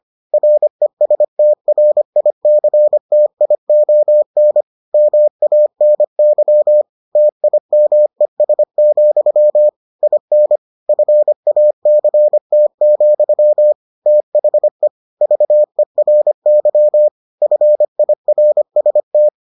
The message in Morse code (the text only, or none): RESTRICTION MANY TIMES, IN FACT, THE VERY FIRST